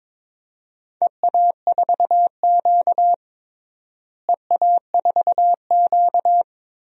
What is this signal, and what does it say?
Morse code, heard as a keyed tone, about 22 wpm, 700 Hz. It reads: EA4Q EA4Q
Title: DXCC-100 Call Signs - 4 Characters - Repeated 2x - 22wpm